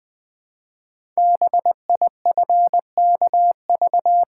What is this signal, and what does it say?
Morse code, heard as a keyed tone, about 20 wpm, 700 Hz. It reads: BIFKV
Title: Code Groups - 5 Letter:Numbers - 20wpm